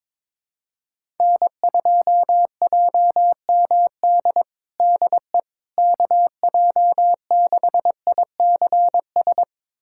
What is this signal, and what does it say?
Morse code, heard as a keyed tone, about 22 wpm, 700 Hz. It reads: N2JMD DE KJ6ICS
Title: USA Callsign DE Callsign 22wpm